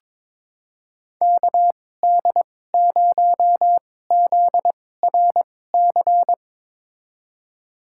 Morse code, heard as a keyed tone, about 21 wpm, spaced as for 16 wpm, 700 Hz. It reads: KD0ZRC
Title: USA Call Signs - 6 Characters - 2x Character Spacing - 22wpm